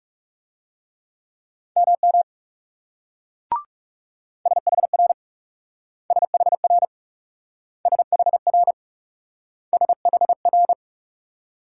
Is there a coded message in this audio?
Yes: Morse code from 1.8 to 10.7 s, heard as a keyed tone, about 39 wpm, 700 Hz. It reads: MM SHR SHR SHR SHR